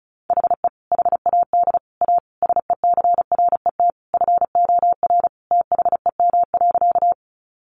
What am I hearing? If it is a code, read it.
Morse code, heard as a keyed tone, about 34 wpm, 700 Hz. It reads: HE HAD A SECRET FOR THEM.
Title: Sentences from Top 1200 Words 35wpm